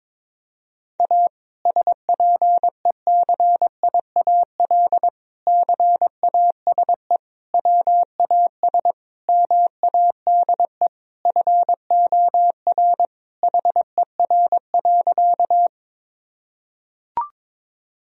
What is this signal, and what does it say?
Morse code, heard as a keyed tone, about 22 wpm, 700 Hz. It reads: A SPECIAL CASE WAS MADE FOR HER.
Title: Sentences from Top 1000 Words - Review Part 3 - 22wpm